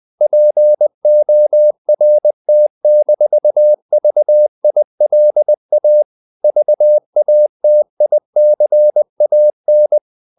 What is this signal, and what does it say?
Morse code, heard as a keyed tone, about 20 wpm, 600 Hz. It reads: PORT-VILA VATICAN
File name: CW / Practice / NationalCapitals20